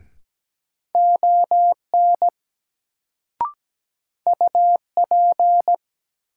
Morse code, heard as a keyed tone, about 17 wpm, 700 Hz. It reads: ON UP
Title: Sentences from Top 100 Words - New Word Drill - 17wpm